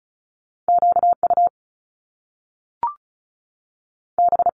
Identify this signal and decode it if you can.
Morse code, heard as a keyed tone, about 36 wpm, spaced as for 6 wpm, 700 Hz. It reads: QUB
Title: Top Two Letter Combinations in Words 35wpm